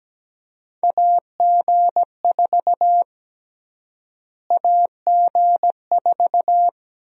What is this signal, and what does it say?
Morse code, heard as a keyed tone, about 17 wpm, 700 Hz. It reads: AG4 AG4
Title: USA Call Sign Prefixes - Repeated 2x - 17wpm